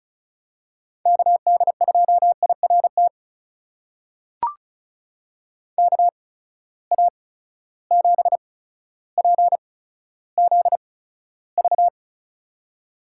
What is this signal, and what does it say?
Morse code, heard as a keyed tone, about 36 wpm, spaced as for 15 wpm, 700 Hz. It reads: KD2IRT KA7PZV
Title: USA Call Signs - 6 Characters - 8x Character Spacing - 35wpm